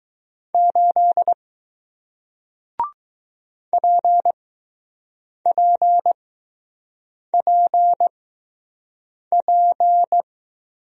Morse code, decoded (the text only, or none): 8 PPPP